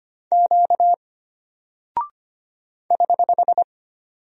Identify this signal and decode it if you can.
Morse code, heard as a keyed tone, about 25 wpm, 700 Hz. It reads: Q <HH>
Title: Single Character (Letter:Number:Punctuation:Prosign) Alt 25wpm